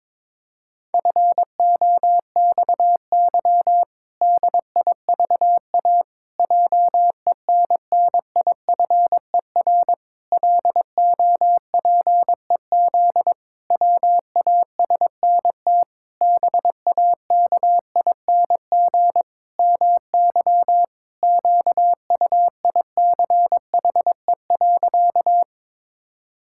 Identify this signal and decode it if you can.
Morse code, heard as a keyed tone, about 22 wpm, 700 Hz. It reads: FOXY DIVA JENNIFER LOPEZ WASNT BAKING MY QUICHE.
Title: Pangram Challenge v2 - 22wpm